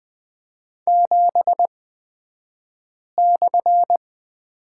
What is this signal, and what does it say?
Morse code, heard as a keyed tone, about 20 wpm, spaced as for 5 wpm, 690 Hz. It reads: 7/
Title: 228 Lesson 37d - Letter x - Two Characters 20wpm